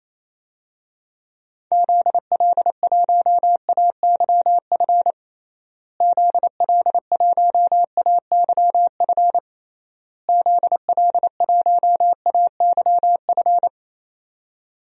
Morse code, heard as a keyed tone, about 27 wpm, 700 Hz. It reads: ZL1AYF ZL1AYF ZL1AYF
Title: DXCC-100 Call Signs - 6 Characters - Repeated 3x - 28wpm